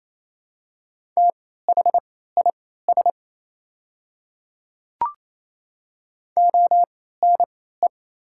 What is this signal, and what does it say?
Morse code, heard as a keyed tone, about 28 wpm, spaced as for 16 wpm, 700 Hz. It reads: THIS ONE
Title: 3x Character Spacing - Top 100 Words - 28wpm